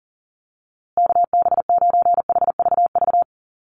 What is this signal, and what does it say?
Morse code, heard as a keyed tone, about 40 wpm, 700 Hz. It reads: KB9HVV
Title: USA Call Signs Repeated 3x 40wpm